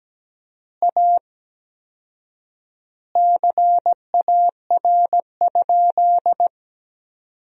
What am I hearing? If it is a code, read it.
Morse code, heard as a keyed tone, about 17 wpm, 700 Hz. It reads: A CAR?
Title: 4x Word-Spacing - Sentences from Top 300 Words - 17wpm